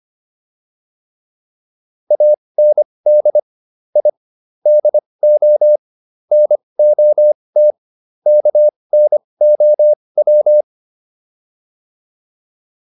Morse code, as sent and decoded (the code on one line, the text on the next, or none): .- -. -.. / .. / -.. --- / -. --- - / -.- -. --- .--
AND I DO NOT KNOW